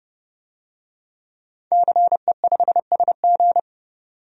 Morse code, heard as a keyed tone, about 30 wpm, 700 Hz. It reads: CE5SG
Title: DXCC-100 Call Signs - 5 Characters - 30wpm